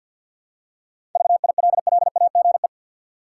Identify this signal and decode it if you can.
Morse code, heard as a keyed tone, about 50 wpm, 700 Hz. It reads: VILLAGE
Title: Seven Letter Words - 50wpm